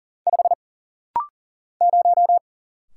Morse code, heard as a keyed tone, about 40 wpm, 700 Hz. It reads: 5 0
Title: Single Letter-Number Alt - Rapid-Fire 40wpm